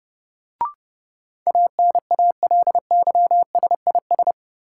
Morse code, heard as a keyed tone, about 32 wpm, 700 Hz. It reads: ANALYSIS
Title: Sentences from Top 1000 Words - Review Part 5 - New Word Drill - 30wpm